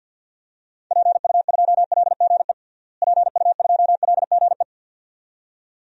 Morse code, heard as a keyed tone, about 52 wpm, 700 Hz. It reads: PU2LGE PU2LGE
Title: DXCC-100 Call Signs - 6 Characters - Repeated 2x - 50wpm